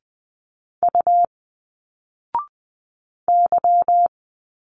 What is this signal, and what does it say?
Morse code, heard as a keyed tone, about 20 wpm, 700 Hz. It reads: U Y
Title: Single Letters 20wpm